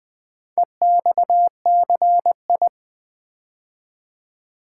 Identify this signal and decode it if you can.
Morse code, heard as a keyed tone, about 20 wpm, 700 Hz. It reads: EXCI
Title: Build A Word - 4 Letters At A Time - 10-Letter Words - 20wpm